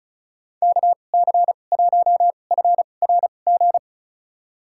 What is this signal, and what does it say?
Morse code, heard as a keyed tone, about 36 wpm, spaced as for 25 wpm, 700 Hz. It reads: KC1FRG
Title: USA Call Signs - 6 Characters - 2x Character Spacing - 35wpm